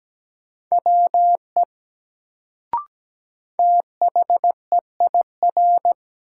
Morse code, heard as a keyed tone, about 17 wpm, 700 Hz. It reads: WE THEIR
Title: Sets of 2 Words - New Word Drill - 17wpm